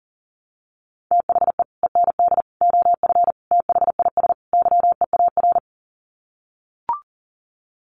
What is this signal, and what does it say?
Morse code, heard as a keyed tone, about 40 wpm, 700 Hz. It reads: THE END OF THIS YEAR
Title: Sets of 5 Words Encore 40wpm